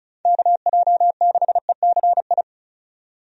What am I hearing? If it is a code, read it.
Morse code, heard as a keyed tone, about 36 wpm, 700 Hz. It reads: KJ6ECI
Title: USA Call Signs - 6 Characters - 35wpm